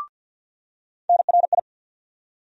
Morse code, heard as a keyed tone, about 47 wpm, 690 Hz. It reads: NRI
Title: USA Call Sign Postfixes - 3 Letters 50wpm